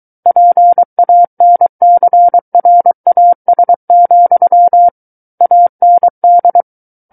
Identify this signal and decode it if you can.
Morse code, heard as a keyed tone, about 23 wpm, 700 Hz. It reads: PANCRAS, AND
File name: Book One / 14-Chapter 14 - 23 WPM